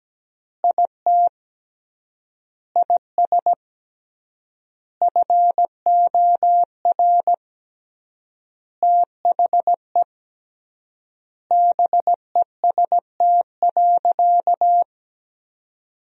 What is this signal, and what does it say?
Morse code, heard as a keyed tone, about 17 wpm, 700 Hz. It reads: IT IS FOR THE BEST.
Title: 3x Word-Spacing - Sentences from Top 500 Words - 17wpm